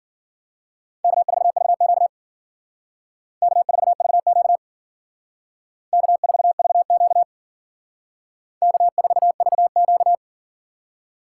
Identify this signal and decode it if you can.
Morse code, heard as a keyed tone, about 49 wpm, 700 Hz. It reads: K4VQ K4VQ K4VQ K4VQ
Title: Speed Racing - USA 1x2 Call Signs 40wpm